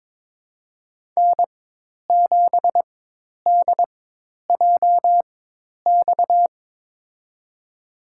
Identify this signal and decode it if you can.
Morse code, heard as a keyed tone, about 22 wpm, spaced as for 10 wpm, 700 Hz. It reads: N7DJX
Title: USA Call Signs - 5 Characters - 4x Character Spacing - 22wpm